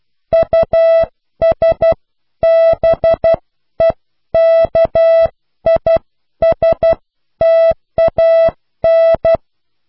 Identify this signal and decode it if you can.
Morse code, heard as a keyed tone, about 12 wpm, spaced as for 10 wpm, 660 Hz. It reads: USBEKISTAN
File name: cw / pseudo worden(10)